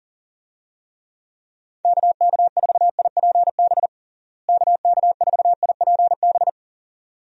Morse code, heard as a keyed tone, about 40 wpm, 700 Hz. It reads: KK4IPB KK4IPB
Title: USA Call Signs - 6 Characters - Repeated 2x - 40wpm